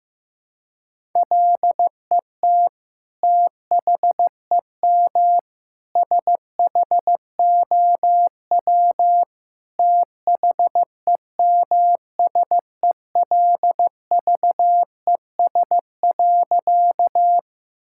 Morse code, heard as a keyed tone, about 14 wpm, 700 Hz. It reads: LET THEM SHOW THEMSELVES.
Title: Sentences from Top 1000 Words - Review Part 5 - 15wpm